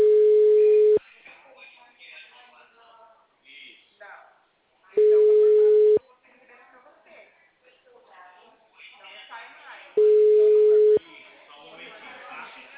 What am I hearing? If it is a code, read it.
Ringback tone (Europe, 425 Hz), with speech.
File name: out- -2011- - - .809